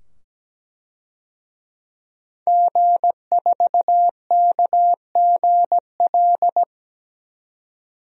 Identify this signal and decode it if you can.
Morse code, heard as a keyed tone, about 17 wpm, 700 Hz. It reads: G4KGL
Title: DXCC-100 Call Signs - 5 Characters - 17wpm